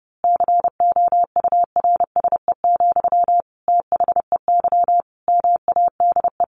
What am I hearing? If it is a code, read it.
Morse code, heard as a keyed tone, about 30 wpm, 700 Hz. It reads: COURSE, THEY MADE
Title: Sentences from Top 1400 Words 30wpm